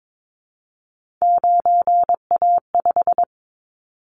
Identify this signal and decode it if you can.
Morse code, heard as a keyed tone, about 22 wpm, 700 Hz. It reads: 9A5
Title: Callsign Prefixes for Top 50 DXCC - 22wpm